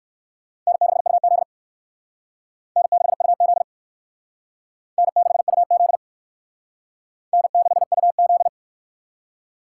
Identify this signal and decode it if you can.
Morse code, heard as a keyed tone, about 56 wpm, 700 Hz. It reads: N6UZ N6UZ N6UZ N6UZ
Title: Speed Racing - USA Call Signs 45wpm